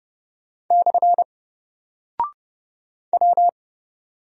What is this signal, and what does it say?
Morse code, heard as a keyed tone, about 30 wpm, 700 Hz. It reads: / W
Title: Single Character (Letter:Number:Punctuation:Prosign) Alt 30wpm